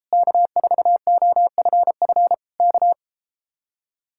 Morse code, heard as a keyed tone, about 33 wpm, 700 Hz. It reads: K4OFF K